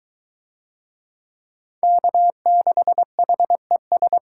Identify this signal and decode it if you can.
Morse code, heard as a keyed tone, about 23 wpm, 700 Hz. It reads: K6HES